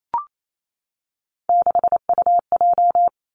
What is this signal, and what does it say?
Morse code, heard as a keyed tone, about 28 wpm, 700 Hz. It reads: 6UJ
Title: Three Challenging Letter-Number Combinations 28wpm